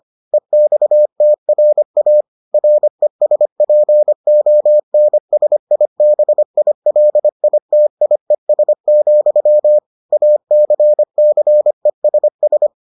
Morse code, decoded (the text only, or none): EXTRA RESPONSIBILITIES, ACCESS